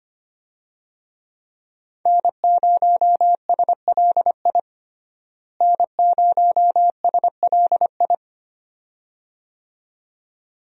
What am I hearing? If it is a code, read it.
Morse code, heard as a keyed tone, about 25 wpm, 700 Hz. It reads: N0SLI N0SLI
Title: USA Call Signs - 5 Characters - Repeated 2x - 25wpm